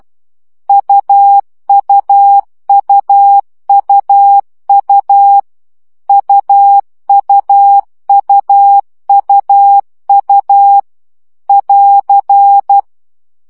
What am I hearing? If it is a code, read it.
Morse code, heard as a keyed tone, about 12 wpm, 800 Hz. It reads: UUUUU UUUUU <AR>
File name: CW gyakorlas u 10 baud